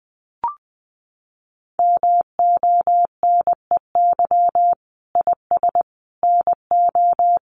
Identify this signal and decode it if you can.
Morse code, heard as a keyed tone, about 20 wpm, 700 Hz. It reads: MONEY IS NO
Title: Sentences from Top 500 Words 20wpm